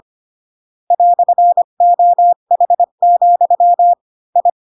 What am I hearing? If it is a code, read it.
Morse code, heard as a keyed tone, about 25 wpm, 700 Hz. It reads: "OH, I
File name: Pinocchio / Pinocchio ch1 20 25 ik1hge